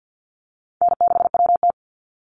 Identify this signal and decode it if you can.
Morse code, heard as a keyed tone, about 48 wpm, 700 Hz. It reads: N6WT
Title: NA SOTA Chaser Callsigns 50wpm